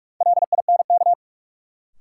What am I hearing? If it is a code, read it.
Morse code, heard as a keyed tone, about 42 wpm, 700 Hz. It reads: LINK
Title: Sentences from Top 1000 Words - Review Part 7 - New Word Drill - 45wpm